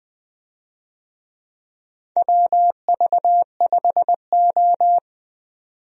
Morse code, heard as a keyed tone, about 21 wpm, 700 Hz. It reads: WV5O
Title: USA Call Signs - 4 Characters - 20wpm